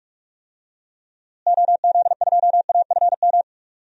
Morse code, heard as a keyed tone, about 46 wpm, 700 Hz. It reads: OZ2AFM
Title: DXCC-100 Call Signs - 6 Characters - Repeated 2x - 45wpm